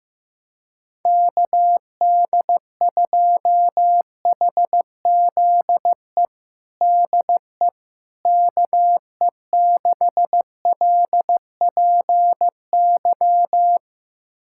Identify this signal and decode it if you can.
Morse code, heard as a keyed tone, about 15 wpm, 700 Hz. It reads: KD2HZE DE KE6LPY
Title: USA Callsign DE Callsign - 6 Characters - 15wpm